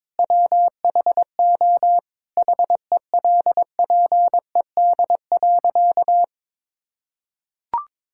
Morse code, heard as a keyed tone, about 22 wpm, 700 Hz. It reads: WHO HELPED.
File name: practice / 22wpm / Sentences with 4-Word Phrases - 22wpm